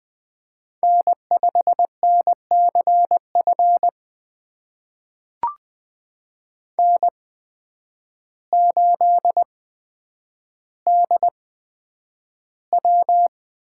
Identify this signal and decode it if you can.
Morse code, heard as a keyed tone, about 20 wpm, spaced as for 9 wpm, 700 Hz. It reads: N5NCF N8DW
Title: USA Call Signs - 5 Characters - 8x Character Spacing - 20wpm